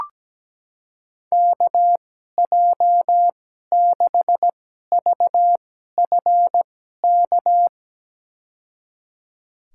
Morse code, heard as a keyed tone, about 17 wpm, spaced as for 12 wpm, 700 Hz. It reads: KJ6VFK
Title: USA Call Signs - 6 Characters - 2x Character Spacing - 17wpm